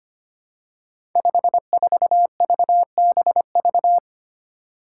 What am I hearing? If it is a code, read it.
Morse code, heard as a keyed tone, about 25 wpm, 700 Hz. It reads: 54VBV
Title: Five Challenging Letter-Number Combinations 25wpm